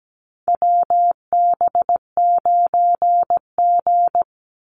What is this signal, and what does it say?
Morse code, heard as a keyed tone, about 17 wpm, 700 Hz. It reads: WB9G
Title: Call Signs - 17wpm